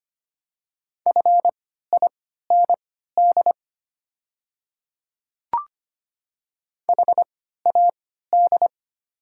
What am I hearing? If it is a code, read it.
Morse code, heard as a keyed tone, about 25 wpm, spaced as for 14 wpm, 700 Hz. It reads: FIND HAD
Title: 3x Character Spacing - Top 300 Words - 25wpm